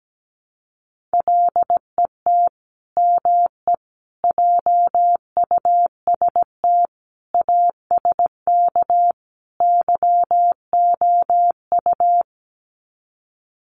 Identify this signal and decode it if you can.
Morse code, heard as a keyed tone, about 17 wpm, 700 Hz. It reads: LET ME JUST ASK YOU
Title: Sets of 5 Words Encore2 17wpm